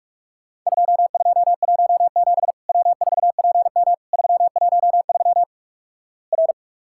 Morse code, heard as a keyed tone, about 46 wpm, 700 Hz. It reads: 2217 W4PM 313 R
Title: Worked All Europe DX Contest - QTC Exchange - 45wpm